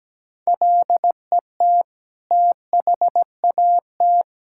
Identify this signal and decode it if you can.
Morse code, heard as a keyed tone, about 17 wpm, 700 Hz. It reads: LET THAT
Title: Sentences from Top 1000 Words - Review Part 5 - 17wpm